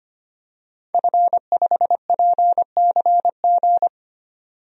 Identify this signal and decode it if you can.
Morse code, heard as a keyed tone, about 25 wpm, 700 Hz. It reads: F5PCG
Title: DXCC-100 Call Signs - 5 Characters - 25wpm